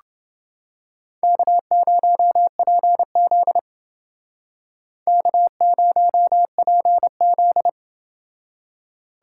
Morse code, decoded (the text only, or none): K0PZ K0PZ